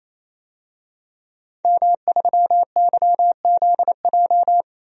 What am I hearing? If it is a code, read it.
Morse code, heard as a keyed tone, about 28 wpm, 700 Hz. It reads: M3YZJ
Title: DXCC-100 Call Signs - 5 Characters - 28wpm